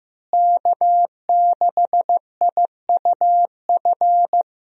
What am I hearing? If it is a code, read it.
Morse code, heard as a keyed tone, about 16 wpm, 700 Hz. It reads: K6IUF
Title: USA Call Signs - 5 Characters - Repeated 2x - 15wpm